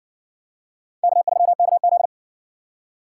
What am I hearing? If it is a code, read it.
Morse code, heard as a keyed tone, about 60 wpm, 700 Hz. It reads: K3KZ